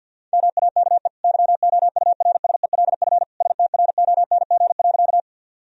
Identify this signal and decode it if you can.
Morse code, heard as a keyed tone, about 50 wpm, 700 Hz. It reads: MAKE YOURSELF STRONG.
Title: Sentences from Top 1000 Words - Review Part 3 - 50wpm